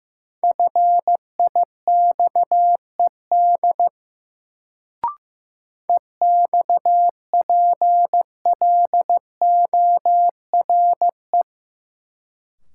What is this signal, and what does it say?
Morse code, heard as a keyed tone, about 15 wpm, 700 Hz. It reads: FIXED EXPLORE
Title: Sentences from Top 1100 Words - New Word Drill - 15wpm